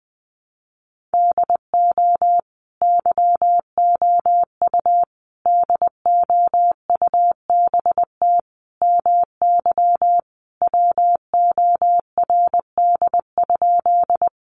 Morse code, heard as a keyed tone, about 20 wpm, 700 Hz. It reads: DO YOU DOUBT MY WORD?
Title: Sentences from Top 1500 Words 20wpm